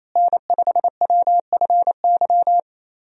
Morse code, heard as a keyed tone, about 28 wpm, 700 Hz. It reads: N5WFY
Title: USA Call Signs - 5 Characters - Repeated 3x - 28wpm